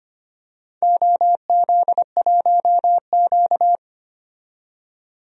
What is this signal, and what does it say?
Morse code, heard as a keyed tone, about 25 wpm, 700 Hz. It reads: OZ1Q
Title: DXCC-100 Call Signs - 4 Characters - 25wpm